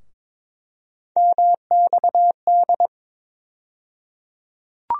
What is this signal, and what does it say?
Morse code, heard as a keyed tone, about 23 wpm, 700 Hz. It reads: MXD
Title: USA Call Sign Postfixes - 3 Letters 22wpm